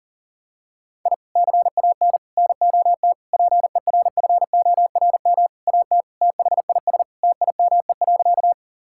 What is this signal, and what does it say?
Morse code, heard as a keyed tone, about 40 wpm, 700 Hz. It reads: I CAN NOT PERFORM AT THIS TIME.
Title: Sentences from Top 1000 Words - Review Part 5 - 40wpm